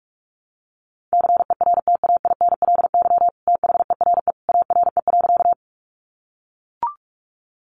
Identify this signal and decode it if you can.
Morse code, heard as a keyed tone, about 44 wpm, 700 Hz. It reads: CERTAINLY THERE ARE.
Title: Sentences from Top 900 Words 45wpm